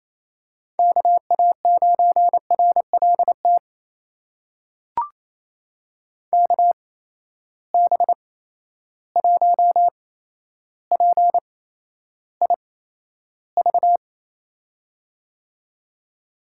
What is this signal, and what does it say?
Morse code, heard as a keyed tone, about 28 wpm, spaced as for 12 wpm, 700 Hz. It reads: KA9RLT KB1PIV
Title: USA Call Signs - 6 Characters - 8x Character Spacing - 28wpm